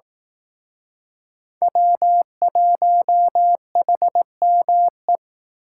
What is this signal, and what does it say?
Morse code, heard as a keyed tone, about 18 wpm, 700 Hz. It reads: W1HME
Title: Speed Racing - USA 1x3 Call Signs 15wpm